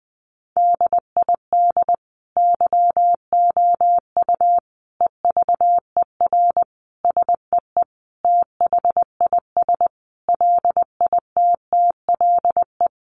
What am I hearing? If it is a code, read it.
Morse code, heard as a keyed tone, about 20 wpm, 700 Hz. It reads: DID YOU EVER SEE THIS LITTLE
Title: Sentences from Top 1500 Words 20wpm